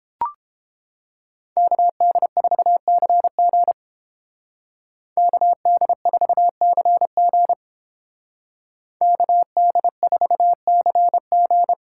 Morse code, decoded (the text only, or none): KD4CG KD4CG KD4CG